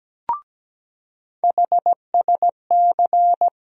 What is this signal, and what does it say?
Morse code, heard as a keyed tone, about 17 wpm, 700 Hz. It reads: HSC
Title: USA Call Sign Postfixes - 3 Letters 17wpm